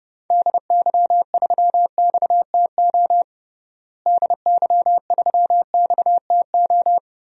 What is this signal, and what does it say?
Morse code, heard as a keyed tone, about 30 wpm, 700 Hz. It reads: DY3XTO DY3XTO
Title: DXCC-100 Call Signs - 6 Characters - Repeated 3x - 30wpm